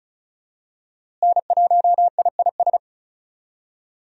Morse code, heard as a keyed tone, about 34 wpm, 700 Hz. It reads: N1IIS
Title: USA Call Signs - 5 Characters - Repeated 3x - 35wpm